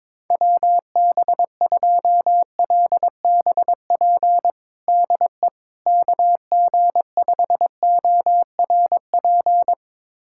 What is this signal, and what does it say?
Morse code, heard as a keyed tone, about 22 wpm, 700 Hz. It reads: WB2LBP DE KG5ORP
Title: USA Callsign DE Callsign - 6 Characters - 22wpm